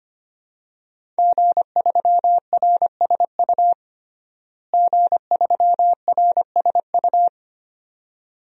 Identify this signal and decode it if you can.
Morse code, heard as a keyed tone, about 25 wpm, 700 Hz. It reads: G3RSU G3RSU
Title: DXCC-100 Call Signs - 5 Characters - Repeated 2x - 25wpm